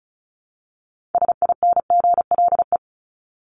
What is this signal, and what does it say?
Morse code, heard as a keyed tone, about 34 wpm, 700 Hz. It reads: SINGLE
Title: Top 501-750 Words 35wpm